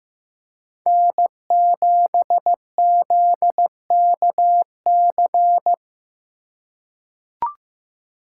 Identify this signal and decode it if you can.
Morse code, heard as a keyed tone, about 15 wpm, 700 Hz. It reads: N7ZKC
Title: USA Call Signs - 5 Characters - 2x Character Spacing - 15wpm